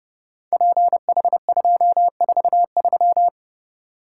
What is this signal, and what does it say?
Morse code, heard as a keyed tone, about 28 wpm, 700 Hz. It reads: PH243
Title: Code Groups - 5 Letter:Numbers - 30wpm